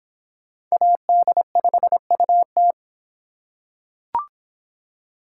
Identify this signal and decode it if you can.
Morse code, heard as a keyed tone, about 25 wpm, 700 Hz. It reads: AD5UT